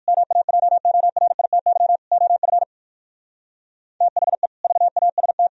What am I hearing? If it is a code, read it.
Morse code, heard as a keyed tone, about 58 wpm, 700 Hz. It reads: MAJORITY OF THE VAST